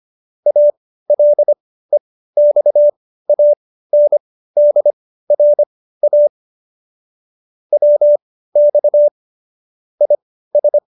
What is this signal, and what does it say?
Morse code, heard as a keyed tone, about 25 wpm, spaced as for 15 wpm, 590 Hz. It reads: ALEXANDRA WX IS
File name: qso219b 15